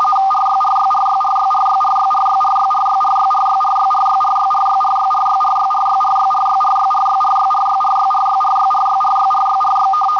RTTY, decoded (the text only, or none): RYRYRYRYRYRYRYRYRYRYRYRYRYRYRYRYRYRYRYRYRYRYRYRYRYRYRYRYRYRYRYRY